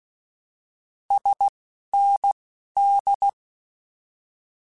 Morse code, heard as a keyed tone, about 16 wpm, spaced as for 11 wpm, 800 Hz. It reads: SND
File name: Abreviations / Abreviations mp3s / Abreviations@10